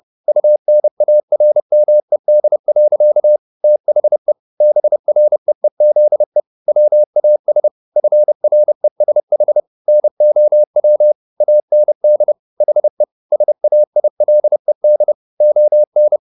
Morse code, heard as a keyed tone, about 30 wpm, 600 Hz. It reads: UNARMED. THE BREEZE WAS FRESH NOW AND HE SAILED ON